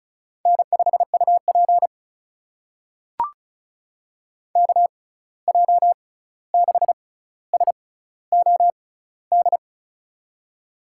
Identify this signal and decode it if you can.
Morse code, heard as a keyed tone, about 36 wpm, spaced as for 16 wpm, 700 Hz. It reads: N5UP KJ6SOD
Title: USA Call Signs - 6x Character Spacing - 35wpm